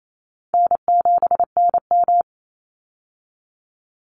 Morse code, heard as a keyed tone, about 28 wpm, 700 Hz. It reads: N7NM
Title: Call Signs Repeated 3x - 28wpm